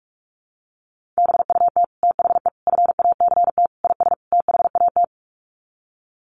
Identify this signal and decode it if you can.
Morse code, heard as a keyed tone, about 44 wpm, 700 Hz. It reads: BUT THE FACT IS THAT
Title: Sets of 5 Words Encore2 45wpm